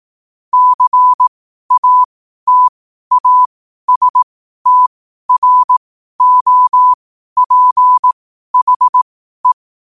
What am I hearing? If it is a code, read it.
Morse code, heard as a keyed tone, about 18 wpm, spaced as for 12 wpm, 1000 Hz. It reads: CATASTROPHE
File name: qotdfiles / 12wpm